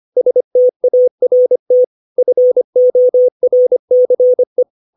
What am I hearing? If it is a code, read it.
Morse code, heard as a keyed tone, about 25 wpm, 500 Hz. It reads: START FORCE